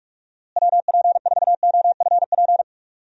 Morse code, heard as a keyed tone, about 46 wpm, 700 Hz. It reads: WP4OFP
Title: USA Call Signs - 6 Characters - Repeated 2x - 45wpm